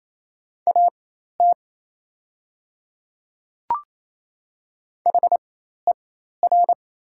Morse code, heard as a keyed tone, about 27 wpm, spaced as for 13 wpm, 700 Hz. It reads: AT HER
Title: 4x Character Spacing - Top 100 Words - 28wpm